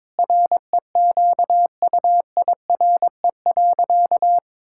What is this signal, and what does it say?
Morse code, heard as a keyed tone, about 22 wpm, 700 Hz. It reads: REQUIRE.
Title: Sentences from Top 1000 Words - Review Part 6 - 22wpm